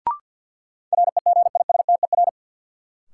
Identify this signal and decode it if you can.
Morse code, heard as a keyed tone, about 53 wpm, 700 Hz. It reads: REGISTER
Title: Sentences from Top 1200 Words - New Word Drill - 50wpm